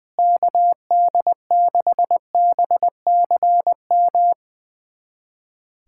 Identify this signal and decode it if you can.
Morse code, heard as a keyed tone, about 21 wpm, 700 Hz. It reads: KD6BCM
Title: USA Callsign DE Callsign - 6 Characters - 20wpm